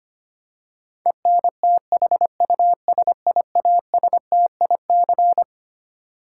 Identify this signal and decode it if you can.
Morse code, heard as a keyed tone, about 25 wpm, 700 Hz. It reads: ENTHUSIASTIC
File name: Twelve Letter Words - 25wpm